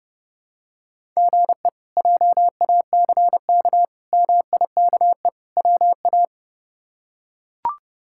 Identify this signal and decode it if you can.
Morse code, heard as a keyed tone, about 30 wpm, 700 Hz. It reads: GE JACK MIKE WA
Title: K1USN SST QSO Practice - 30wpm